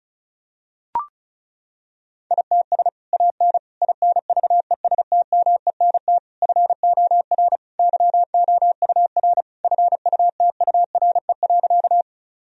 Morse code, heard as a keyed tone, about 34 wpm, 700 Hz. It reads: ITS AN INVESTMENT FOR YOUR FUTURE.
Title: Sentences from Top 1000 Words - Review Part 4 - 35wpm